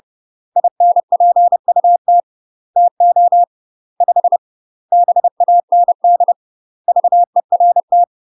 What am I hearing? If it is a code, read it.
Morse code, heard as a keyed tone, about 30 wpm, 700 Hz. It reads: INPUT TO 5 BAND VERT
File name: UK MASTER QSOs TRAINING / 30WPM / UK MASTER QSOs 0046